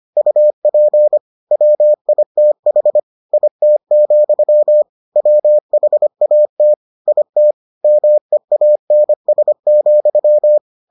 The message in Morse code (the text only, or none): UP WITH IT, WHAT IT MEANS,